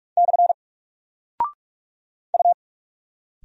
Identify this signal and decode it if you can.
Morse code, heard as a keyed tone, about 46 wpm, 700 Hz. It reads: / U
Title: Single Character (Letter:Number:Punctuation:Prosign) Alt 45wpm